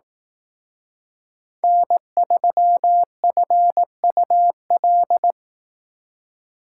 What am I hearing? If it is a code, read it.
Morse code, heard as a keyed tone, about 18 wpm, 700 Hz. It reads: N3FUL
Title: Speed Racing - USA 1x3 Call Signs 15wpm